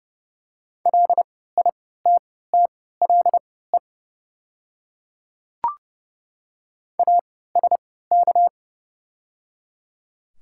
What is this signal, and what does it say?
Morse code, heard as a keyed tone, about 29 wpm, spaced as for 17 wpm, 700 Hz. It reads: LITTLE ASK
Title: 3x Character Spacing - Top 200 Words - 30wpm